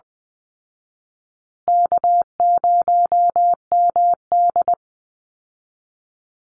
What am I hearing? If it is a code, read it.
Morse code, heard as a keyed tone, about 20 wpm, 700 Hz. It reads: K0MD
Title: Easy Call Signs Repeated 3x - 20wpm